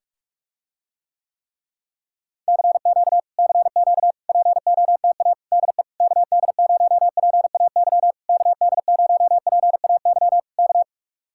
Morse code, heard as a keyed tone, about 44 wpm, 700 Hz. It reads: CQ CQ POTA DE KD0PAY KD0PAY K